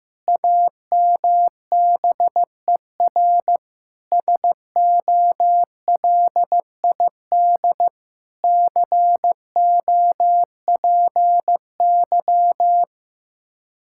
Morse code, heard as a keyed tone, about 15 wpm, 700 Hz. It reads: AMBER SOLID COPY
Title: Practice QSO - Exchange 3 - More Info - 15wpm